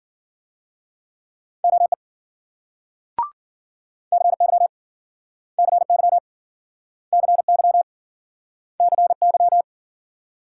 Morse code, heard as a keyed tone, about 56 wpm, 700 Hz. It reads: OE CY CY CY CY